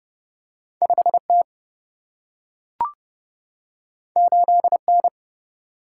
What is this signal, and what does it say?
Morse code, heard as a keyed tone, about 32 wpm, 700 Hz. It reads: 5 T 8 N
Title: Single-Letter Single-Number 30wpm